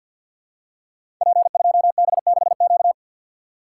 Morse code, heard as a keyed tone, about 51 wpm, 700 Hz. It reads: P2B6Q
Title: Code Groups - 5 Letter:Numbers - 50wpm